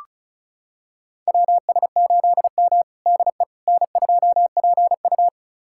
Morse code, heard as a keyed tone, about 36 wpm, 700 Hz. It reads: WS8M DE N2PU
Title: USA Callsign DE Callsign - 4 Characters - 35wpm